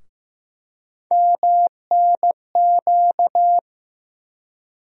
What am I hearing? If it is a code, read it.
Morse code, heard as a keyed tone, about 15 wpm, 700 Hz. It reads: MNQ
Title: USA Call Sign Postfixes - 3 Letters 15wpm